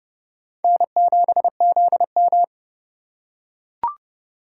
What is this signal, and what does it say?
Morse code, heard as a keyed tone, about 28 wpm, 700 Hz. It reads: N7ZM